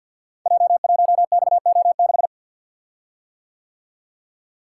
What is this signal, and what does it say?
Morse code, heard as a keyed tone, about 52 wpm, 700 Hz. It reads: J1XO6
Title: Code Groups - 5 Letter:Numbers - 50wpm